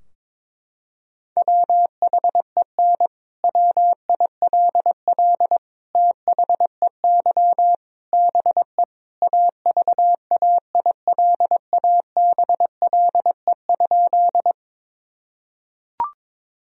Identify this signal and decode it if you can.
Morse code, heard as a keyed tone, about 22 wpm, 700 Hz. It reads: WHEN WILL THEY BE AVAILABLE?
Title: Sentences from Top 1000 Words - Review Part 3 - 22wpm